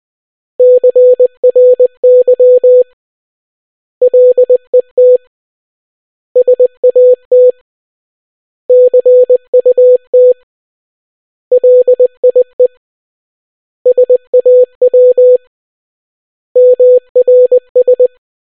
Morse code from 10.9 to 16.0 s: LIE SAW